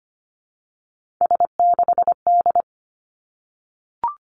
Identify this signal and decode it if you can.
Morse code, heard as a keyed tone, about 25 wpm, 700 Hz. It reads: S6D
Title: Three Challenging Letter-Number Combinations 25wpm